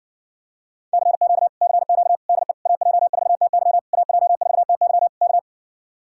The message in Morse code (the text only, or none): CQ CQ DE AJ4TY AJ4TY K